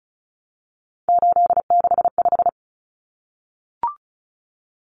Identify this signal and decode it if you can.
Morse code, heard as a keyed tone, about 34 wpm, 700 Hz. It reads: 865
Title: Common Cut Numbers - 3-Digit 35wpm